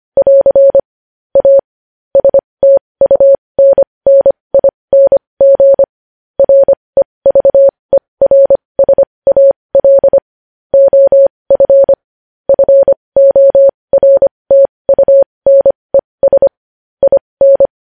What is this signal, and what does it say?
Morse code, heard as a keyed tone, about 25 wpm, spaced as for 20 wpm, 550 Hz. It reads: <AR> A STUNNING REVERSAL OF FORTUNES IN
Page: news headlines in morse code at 20 wpm